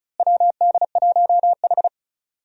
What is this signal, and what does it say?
Morse code, heard as a keyed tone, about 36 wpm, 700 Hz. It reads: WD1H
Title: USA Call Signs - 4 Characters - Repeated 2x - 35wpm